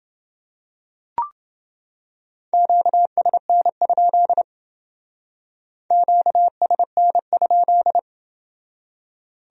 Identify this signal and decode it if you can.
Morse code, heard as a keyed tone, about 28 wpm, 700 Hz. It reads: QSN? QSN?